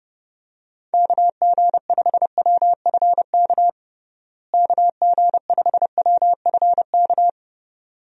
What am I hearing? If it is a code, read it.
Morse code, heard as a keyed tone, about 30 wpm, 700 Hz. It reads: KG5WFK KG5WFK
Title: USA Call Signs - 6 Characters - Repeated 2x - 30wpm